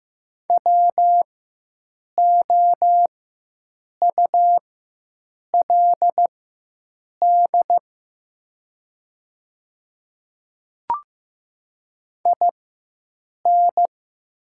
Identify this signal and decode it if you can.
Morse code, heard as a keyed tone, about 15 wpm, spaced as for 7 wpm, 700 Hz. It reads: WOULD IN
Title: 4x Character Spacing - Top 100 Words - 15wpm